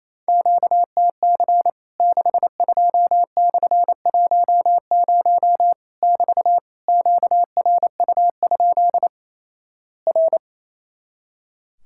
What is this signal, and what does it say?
Morse code, heard as a keyed tone, about 28 wpm, 700 Hz. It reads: QTC 62/10 <BT> QRU? R
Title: Worked All Europe DX Contest - QTC Exchange - 28wpm